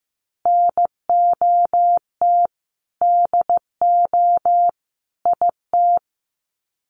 Morse code, heard as a keyed tone, about 15 wpm, 700 Hz. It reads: NOT DO IT
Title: Sets of 5 Words Encore2 15wpm